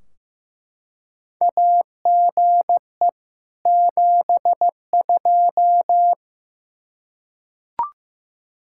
Morse code, heard as a keyed tone, about 15 wpm, 700 Hz. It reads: AGE 72
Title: K1USN SST QSO Practice 15wpm